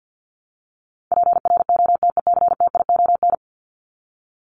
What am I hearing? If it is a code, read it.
Morse code, heard as a keyed tone, about 48 wpm, 700 Hz. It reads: PROTECTION
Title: Top 1501-1750 Words 50wpm